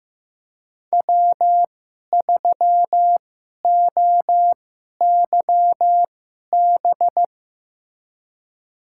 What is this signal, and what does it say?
Morse code, heard as a keyed tone, about 15 wpm, spaced as for 11 wpm, 700 Hz. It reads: W3OYB
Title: USA Call Signs - 5 Characters - 2x Character Spacing - 15wpm